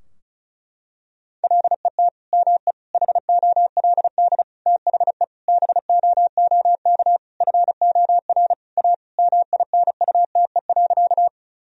Morse code, heard as a keyed tone, about 36 wpm, 700 Hz. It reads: LET ME HOLD THE BOOK FOR A MINUTE.
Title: Sentences from Top 1000 Words - Review Part 2 - 35wpm